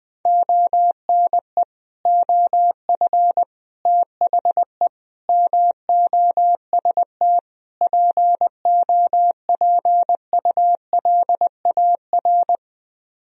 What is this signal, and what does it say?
Morse code, heard as a keyed tone, about 20 wpm, 700 Hz. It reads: ONE OF THE MOST POPULAR
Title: Speed-Racing - Sets of 5 Words 17wpm